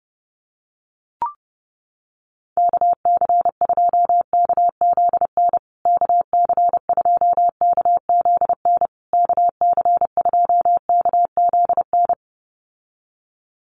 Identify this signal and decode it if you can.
Morse code, heard as a keyed tone, about 28 wpm, 700 Hz. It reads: KC2KZN KC2KZN KC2KZN
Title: USA Call Signs Repeated 3x 30wpm